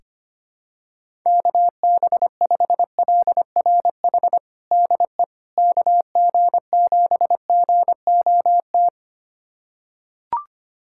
Morse code, heard as a keyed tone, about 25 wpm, 700 Hz. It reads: KB5LRH DE KG7GOT
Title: USA Callsign DE Callsign 25wpm